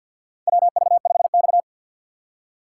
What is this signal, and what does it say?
Morse code, heard as a keyed tone, about 51 wpm, 700 Hz. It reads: WV5X
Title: USA Call Signs - 4 Characters - 2x Character Spacing - 50wpm